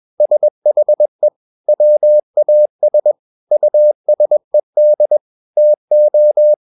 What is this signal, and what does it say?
Morse code, heard as a keyed tone, about 21 wpm, 600 Hz. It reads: SHE WAS USED TO
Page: CW med Gnister